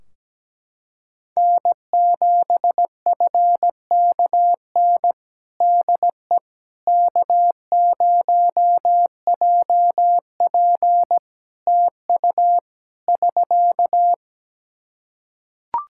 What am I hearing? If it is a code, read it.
Morse code, heard as a keyed tone, about 17 wpm, 700 Hz. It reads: N7FKN DE K0JP TU <SK>
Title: Practice QSO - Exchange 4 - Ending - 17wpm